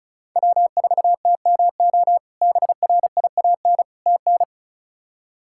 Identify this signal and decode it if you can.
Morse code, heard as a keyed tone, about 36 wpm, 700 Hz. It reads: W4TMO BRIAN TN
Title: K1USN SST QSO Practice - 35wpm